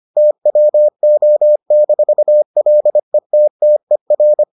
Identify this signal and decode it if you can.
Morse code, heard as a keyed tone, about 25 wpm, 600 Hz. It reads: TWO-LETTER